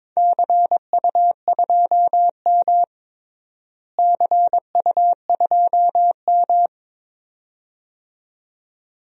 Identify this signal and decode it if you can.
Morse code, heard as a keyed tone, about 21 wpm, 700 Hz. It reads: CU2M CU2M
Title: DXCC-100 Call Signs - 4 Characters - Repeated 2x - 22wpm